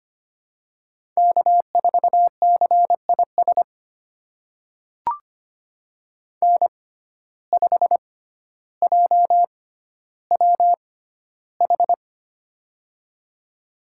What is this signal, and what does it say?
Morse code, heard as a keyed tone, about 25 wpm, spaced as for 13 wpm, 700 Hz. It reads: K4CIS N5JWH
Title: USA Call Signs - 5 Characters - 6x Character Spacing - 25wpm